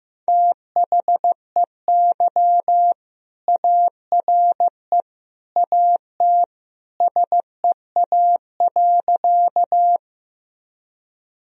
Morse code, heard as a keyed tone, about 14 wpm, 700 Hz. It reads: THEY ARE AT SEA.
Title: Sentences from Top 1000 Words - Review Part 1 - 15wpm